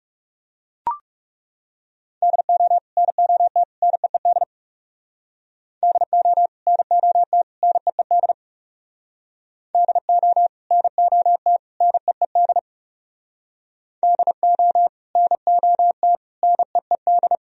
Morse code, heard as a keyed tone, about 35 wpm, 690 Hz. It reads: DO NOT NEED DO NOT NEED DO NOT NEED DO NOT NEED
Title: Speed-Racing - Sets of 3 Words 30wpm